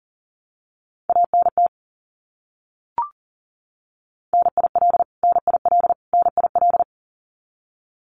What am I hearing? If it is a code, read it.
Morse code, heard as a keyed tone, about 40 wpm, 700 Hz. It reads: ANT NIL NIL NIL
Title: Abbreviations Repeated 3x - 40wpm